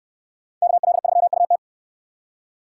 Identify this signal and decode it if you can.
Morse code, heard as a keyed tone, about 67 wpm, 700 Hz. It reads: KL3UT